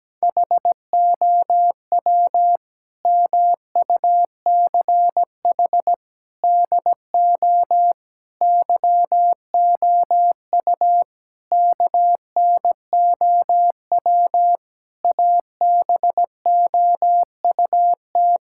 Morse code, heard as a keyed tone, about 17 wpm, 700 Hz. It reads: HOW MUCH DO YOU KNOW ABOUT
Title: Sentences from Top 1000 Words - Review Part 6 - 17wpm